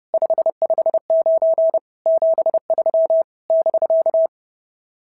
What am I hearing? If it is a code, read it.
Morse code, heard as a keyed tone, about 28 wpm, 660 Hz. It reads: 559 73 <BK>
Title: SOTA QSO Practice 30wpm